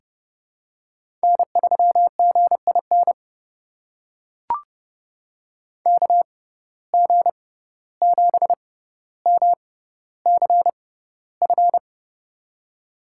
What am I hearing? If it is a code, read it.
Morse code, heard as a keyed tone, about 31 wpm, spaced as for 15 wpm, 700 Hz. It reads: N3GIN KG7MCF
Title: USA Call Signs - 6x Character Spacing - 30wpm